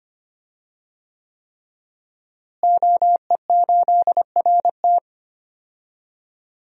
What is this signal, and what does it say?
Morse code, heard as a keyed tone, about 25 wpm, 700 Hz. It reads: OE8RT
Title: DXCC-100 Call Signs - 5 Characters - Repeated 3x - 25wpm